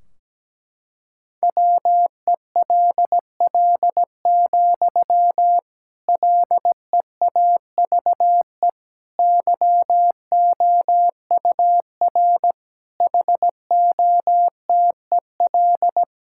Morse code, heard as a keyed tone, about 17 wpm, 700 Hz. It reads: WELL, LEAVE YOUR HOTEL
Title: Sentences from Top 1600 Words 17wpm